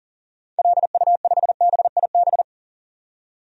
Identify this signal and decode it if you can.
Morse code, heard as a keyed tone, about 40 wpm, 700 Hz. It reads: LU5BIB
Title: DXCC-100 Call Signs - 6 Characters - 40wpm